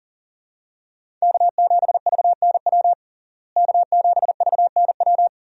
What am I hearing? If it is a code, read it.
Morse code, heard as a keyed tone, about 40 wpm, 700 Hz. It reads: K7VNW K7VNW
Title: USA Call Signs - 5 Characters - Repeated 2x - 40wpm